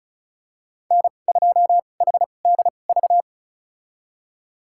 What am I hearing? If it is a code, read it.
Morse code, heard as a keyed tone, about 36 wpm, spaced as for 25 wpm, 700 Hz. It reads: N2HDV
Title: USA Call Signs - 2x Character Spacing - 35wpm